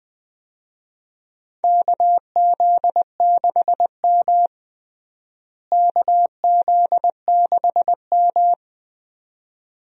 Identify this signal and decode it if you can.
Morse code, heard as a keyed tone, about 20 wpm, 700 Hz. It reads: KZ6M KZ6M
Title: USA Call Signs - 4 Characters - Repeated 2x - 20wpm